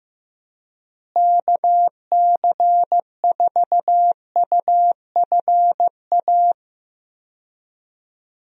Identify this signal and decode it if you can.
Morse code, heard as a keyed tone, about 15 wpm, 700 Hz. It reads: KC4UFA
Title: USA Call Signs - 6 Characters - Repeated 2x - 15wpm